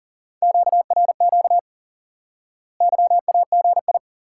Morse code, heard as a keyed tone, about 40 wpm, 700 Hz. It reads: QRQ YAGI
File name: IQE Course 3 - Lesson 89 - YAGI - Two Elements - 40wpm